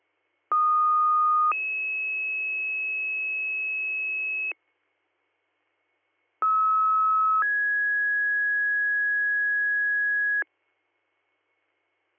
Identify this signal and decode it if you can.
Two-tone page, heard as a one-second tone, then a three-second tone, about 1230 then 2600 Hz, about 1290 then 1650 Hz. It is heard from 0.5 to 10.4 s.